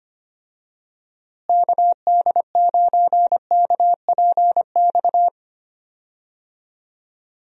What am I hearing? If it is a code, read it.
Morse code, heard as a keyed tone, about 25 wpm, 700 Hz. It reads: KD9KPX
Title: USA Call Signs - 6 Characters - Repeated 3x - 25wpm